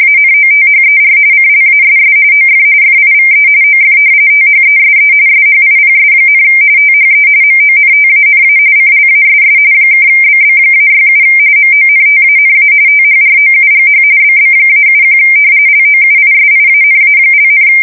RTTY, about 45.45 baud. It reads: ANOTHER CLASS OF FIVE OF THESE SHIPS, THE MONTANA CLASS (BB-67 MONTANA THROUGH BB-71 LOUISIANA),
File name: fsk170 / battleships